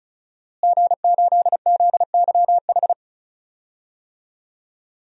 Morse code, heard as a keyed tone, about 34 wpm, 700 Hz. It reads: G8ZYH
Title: DXCC-100 Call Signs - 5 Characters - Repeated 3x - 35wpm